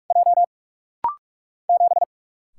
Morse code, heard as a keyed tone, about 44 wpm, 700 Hz. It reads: J 7
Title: Single Letter-Number - Rapid-Fire 45wpm